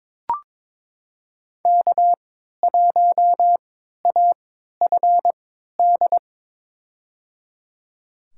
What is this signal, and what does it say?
Morse code, heard as a keyed tone, about 21 wpm, spaced as for 12 wpm, 700 Hz. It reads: K1AFD
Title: USA Call Signs - 5 Characters - 3x Character Spacing - 22wpm